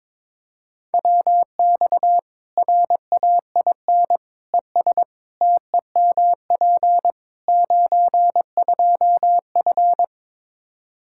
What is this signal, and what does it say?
Morse code, heard as a keyed tone, about 22 wpm, 700 Hz. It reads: WX RAIN ES TEMP 92F
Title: Practice QSO - Exchange 2 - Info - 22wpm